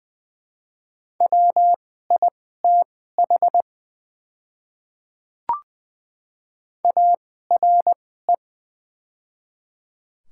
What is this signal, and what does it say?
Morse code, heard as a keyed tone, about 20 wpm, spaced as for 14 wpm, 700 Hz. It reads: WITH ARE
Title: 2x Character Spacing - Top 100 Words - 20wpm